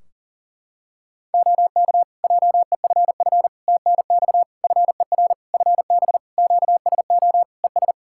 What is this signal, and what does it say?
Morse code, heard as a keyed tone, about 40 wpm, 700 Hz. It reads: OK JEFF TNX FER FB QSO ES
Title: Practice QSO - Exchange 4 - Ending - 40wpm